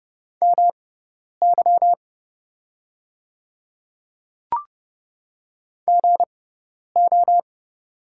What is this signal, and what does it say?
Morse code, heard as a keyed tone, about 30 wpm, spaced as for 10 wpm, 690 Hz. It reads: MY GO
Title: 6x Character Spacing - Top 500 Words - 30wpm